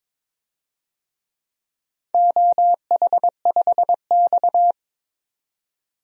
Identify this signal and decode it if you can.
Morse code, heard as a keyed tone, about 22 wpm, 700 Hz. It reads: OH5X
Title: DXCC-100 Call Signs - 4 Characters - 22wpm